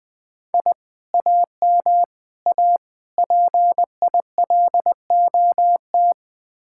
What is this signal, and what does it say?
Morse code, heard as a keyed tone, about 20 wpm, 700 Hz. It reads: I AM A PILOT
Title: Speed-Racing Common QSO Phrases - 17wpm